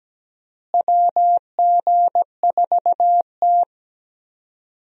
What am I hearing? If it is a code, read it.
Morse code, heard as a keyed tone, about 17 wpm, 700 Hz. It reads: WG4T
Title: USA Call Signs - 4 Characters - Repeated 2x - 17wpm